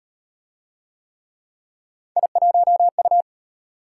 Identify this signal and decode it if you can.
Morse code, heard as a keyed tone, about 34 wpm, 700 Hz. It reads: I1U